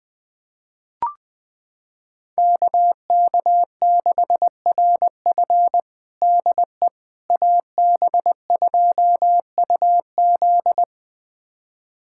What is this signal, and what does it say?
Morse code, heard as a keyed tone, about 20 wpm, 700 Hz. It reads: KK6RF DE AB2UZ
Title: USA Callsign DE Callsign - 5 Characters - 20wpm